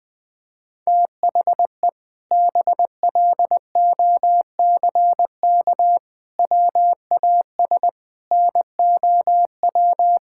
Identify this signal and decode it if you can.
Morse code, heard as a keyed tone, about 21 wpm, 700 Hz. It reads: THE BLOCK WAS NOW
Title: Sentences from Top 1600 Words 20wpm